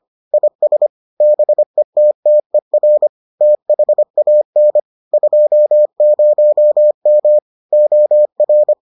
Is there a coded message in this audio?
Yes: Morse code from 0.3 to 8.7 s, heard as a keyed tone, about 25 wpm, 600 Hz. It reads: IS BETTER THAN 20M OR